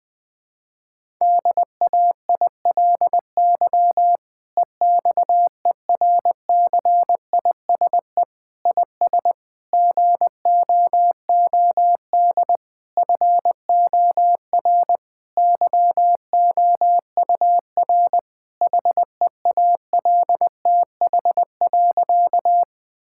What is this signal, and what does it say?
Morse code, heard as a keyed tone, about 21 wpm, 700 Hz. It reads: DAILY EXERCISE IS GOOD FOR YOUR HEALTH.
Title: Difficult Sentences with Hints - 20wpm